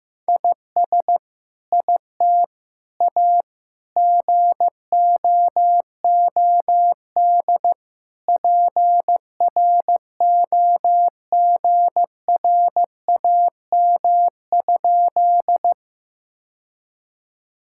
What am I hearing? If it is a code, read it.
Morse code, heard as a keyed tone, about 15 wpm, 700 Hz. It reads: IS IT A GOOD PROGRAM?
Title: Sentences from Top 1000 Words - Review Part 5 - 15wpm